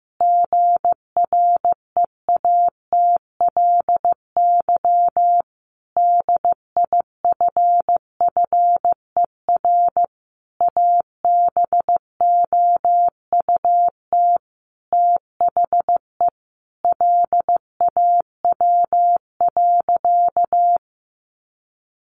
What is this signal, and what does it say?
Morse code, heard as a keyed tone, about 15 wpm, 700 Hz. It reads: GREATLY DIFFER ABOUT THE LAW.
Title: Sentences from Top 1400 Words 15wpm